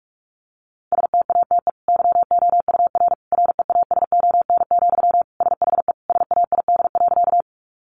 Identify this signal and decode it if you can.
Morse code, heard as a keyed tone, about 44 wpm, 700 Hz. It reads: STATE YOUR REASON, SHE SAID.
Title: Sentences from Top 700 Words 45wpm